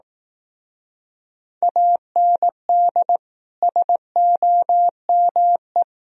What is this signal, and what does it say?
Morse code, heard as a keyed tone, about 18 wpm, 700 Hz. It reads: AND SOME
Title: Speed-Racing - Sets of 4 Words 15wpm